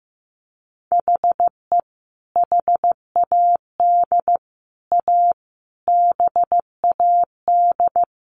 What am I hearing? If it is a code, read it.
Morse code, heard as a keyed tone, about 15 wpm, 700 Hz. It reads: HE HAD A BAD
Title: Sentences from Top 1500 Words 15wpm